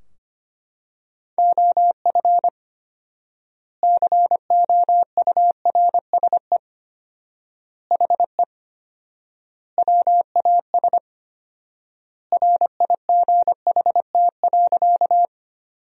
Morse code, heard as a keyed tone, about 25 wpm, 700 Hz. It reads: OF COURSE HE WAS RIGHT.
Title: 4x Word-Spacing - Sentences from Top 500 Words - 25wpm